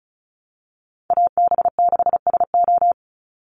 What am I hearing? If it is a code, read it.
Morse code, heard as a keyed tone, about 34 wpm, 700 Hz. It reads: AB6SO
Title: NA SOTA Chaser Callsigns 35wpm